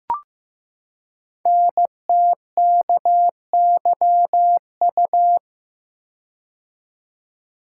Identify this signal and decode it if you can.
Morse code, heard as a keyed tone, about 15 wpm, 700 Hz. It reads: NTKYU
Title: Code Groups - 5 Letters - 15wpm